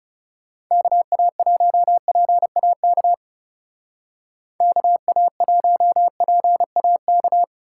Morse code, heard as a keyed tone, about 36 wpm, 700 Hz. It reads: KA1PAK KA1PAK
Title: Speed Racing - USA 2x3 Call Signs 30wpm